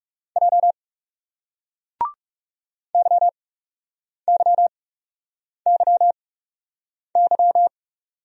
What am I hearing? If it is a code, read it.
Morse code, heard as a keyed tone, about 39 wpm, spaced as for 8 wpm, 700 Hz. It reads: J YYYY